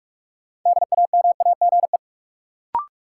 Morse code, heard as a keyed tone, about 46 wpm, 700 Hz. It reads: DAMAGE
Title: Sentences from Top 1100 Words - New Word Drill Alt - 45wpm